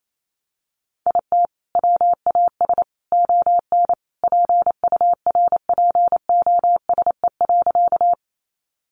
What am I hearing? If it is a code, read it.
Morse code, heard as a keyed tone, about 28 wpm, 700 Hz. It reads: IT WAS ON PURPOSE.
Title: Sentences from Top 800 Words 28wpm